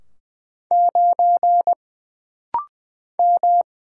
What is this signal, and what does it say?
Morse code, heard as a keyed tone, about 20 wpm, 700 Hz. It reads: 9 M
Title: Single Letter-Number Alt - Rapid-Fire 20wpm